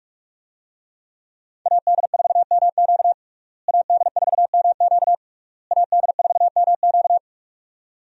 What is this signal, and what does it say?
Morse code, heard as a keyed tone, about 46 wpm, 700 Hz. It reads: AD4MQ AD4MQ AD4MQ
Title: USA Call Signs - 5 Characters - Repeated 3x - 45wpm